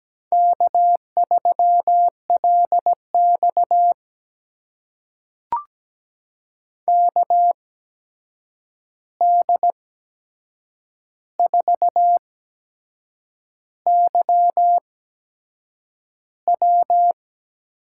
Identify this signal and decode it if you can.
Morse code, heard as a keyed tone, about 17 wpm, spaced as for 7 wpm, 700 Hz. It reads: K3LX KD4YW
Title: USA Call Signs - 8x Character Spacing - 17wpm